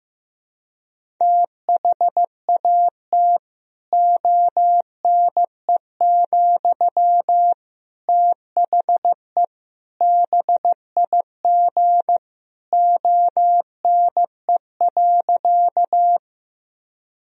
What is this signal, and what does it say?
Morse code, heard as a keyed tone, about 15 wpm, 700 Hz. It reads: THAT ONE, THE BIG ONE.
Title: Sentences from Top 1000 Words - Review Part 1 - 15wpm